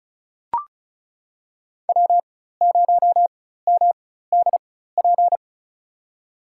Morse code, heard as a keyed tone, about 36 wpm, spaced as for 16 wpm, 700 Hz. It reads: W0MDP
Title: USA Call Signs - 4x Character Spacing - 35wpm